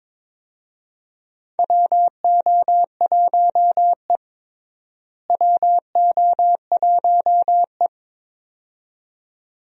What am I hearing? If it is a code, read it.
Morse code, heard as a keyed tone, about 22 wpm, 700 Hz. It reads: WO1E WO1E
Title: USA Call Signs - 4 Characters - Repeated 2x - 22wpm